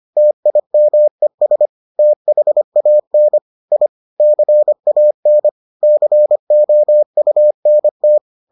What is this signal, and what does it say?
Morse code, heard as a keyed tone, about 25 wpm, 600 Hz. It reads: TIMES THAN I CAN COUNT